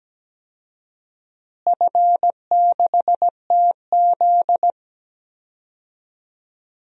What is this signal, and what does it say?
Morse code, heard as a keyed tone, about 17 wpm, 700 Hz. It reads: F6TZ
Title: DXCC-100 Call Signs - 4 Characters - Repeated 3x - 17wpm